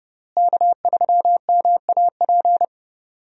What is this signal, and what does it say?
Morse code, heard as a keyed tone, about 30 wpm, 700 Hz. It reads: K3MAP
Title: USA Call Signs - 5 Characters - Repeated 2x - 30wpm